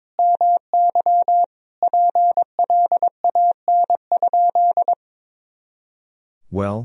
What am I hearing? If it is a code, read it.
Morse code, heard as a keyed tone, about 22 wpm, 700 Hz. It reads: MY PLAN?
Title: Sentences from Top 500 Words and Abbreviations 22wpm